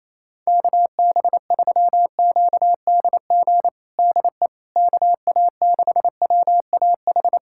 Morse code, heard as a keyed tone, about 28 wpm, 700 Hz. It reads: KB3QDG DE KA6WAH
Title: USA Callsign DE Callsign - 6 Characters - 28wpm